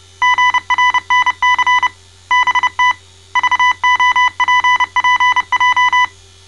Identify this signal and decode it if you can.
Morse code, heard as a keyed tone, about 29 wpm, 1000 Hz. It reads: GRNC BT VOPPJ
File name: mrx / malta